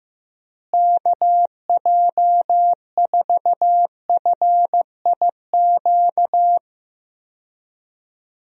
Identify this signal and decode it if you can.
Morse code, heard as a keyed tone, about 15 wpm, 700 Hz. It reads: KJ4FIQ
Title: USA Call Signs - 6 Characters - Repeated 3x - 15wpm